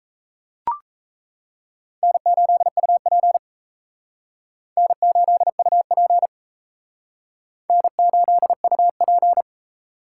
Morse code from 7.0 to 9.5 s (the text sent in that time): N8UP